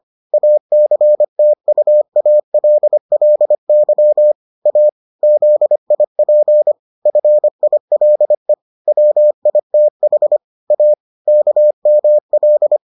Morse code, heard as a keyed tone, about 25 wpm, 600 Hz. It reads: ACTUALLY A ZIP FILE WITH A KML